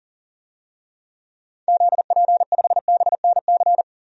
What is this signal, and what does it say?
Morse code, heard as a keyed tone, about 40 wpm, 700 Hz. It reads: ZP5BNC
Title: DXCC-100 Call Signs - 6 Characters - 40wpm